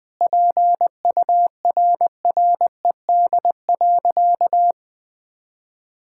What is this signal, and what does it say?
Morse code, heard as a keyed tone, about 20 wpm, 700 Hz. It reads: PURRED.
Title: Sentences with 2-Word Phrases - Spoken Beginning - 20wpm